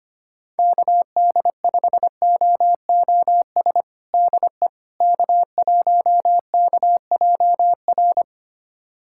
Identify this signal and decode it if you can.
Morse code, heard as a keyed tone, about 25 wpm, 700 Hz. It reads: KD5OOS DE K1KJR
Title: USA Callsign DE Callsign 25wpm